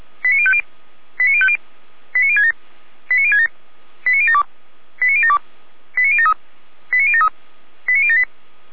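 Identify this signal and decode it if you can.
Five-tone selective call (ZVEI), heard as quick runs of five tones, tones of about 70 ms. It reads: 89050 89050 89067 89067 89072 89072 89073 89073 89078
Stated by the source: ZVEI five-tone call received by SDR, server of Feuerwehr Orlamünde fire brigade.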